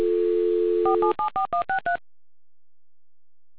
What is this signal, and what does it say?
DTMF, heard as telephone touch tones, tones of about 110 ms. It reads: 4774163